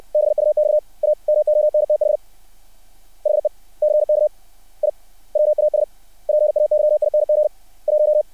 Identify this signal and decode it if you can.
Morse code, heard as a keyed tone, about 60 wpm, 600 Hz. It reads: BUX A MONTH HECK I WAS PAYING 1
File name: archived / CRSnet